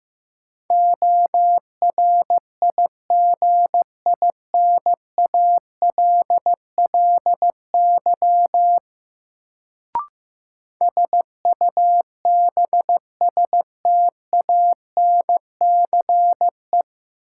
Morse code, heard as a keyed tone, about 15 wpm, 700 Hz. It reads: ORIGINALLY SUBSTANCE
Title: Sentences from Top 1500 Words - New Word Drill - 15wpm